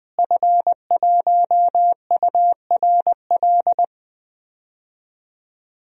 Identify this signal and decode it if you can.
Morse code, heard as a keyed tone, about 20 wpm, 700 Hz. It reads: F1URL
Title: DXCC-100 Call Signs - 5 Characters - Repeated 3x - 20wpm